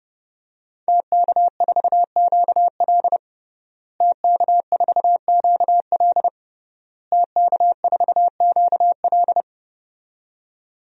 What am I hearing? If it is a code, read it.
Morse code, heard as a keyed tone, about 30 wpm, 700 Hz. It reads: TK4QL TK4QL TK4QL
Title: DXCC-100 Call Signs - 5 Characters - Repeated 3x - 30wpm